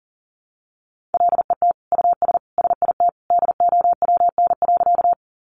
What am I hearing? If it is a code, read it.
Morse code, heard as a keyed tone, about 40 wpm, 700 Hz. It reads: LET US SIT DOWN.
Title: Sentences from Top 500 Words 40wpm